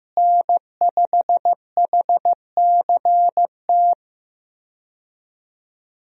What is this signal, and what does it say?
Morse code, heard as a keyed tone, about 15 wpm, 700 Hz. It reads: N5HCT
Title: USA Call Signs - 5 Characters - Repeated 3x - 15wpm